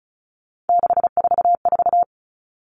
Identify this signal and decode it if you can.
Morse code, heard as a keyed tone, about 34 wpm, 700 Hz. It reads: 644
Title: Numbers - 3-Digit 35wpm